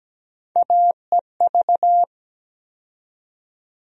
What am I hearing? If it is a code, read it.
Morse code, heard as a keyed tone, about 17 wpm, 700 Hz. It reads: AEV
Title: USA Call Sign Postfixes - 3 Letters 17wpm